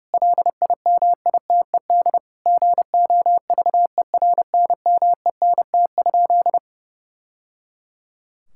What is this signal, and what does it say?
Morse code, heard as a keyed tone, about 30 wpm, 700 Hz. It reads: LIMITED GOVERNMENT?
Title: Sentences from Top 1000 Words - Review Part 6 - 30wpm